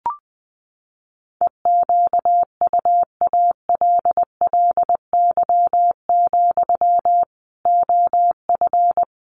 Morse code, heard as a keyed tone, about 20 wpm, 700 Hz. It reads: EQUALLY, OF
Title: Sentences from Top 1400 Words 20wpm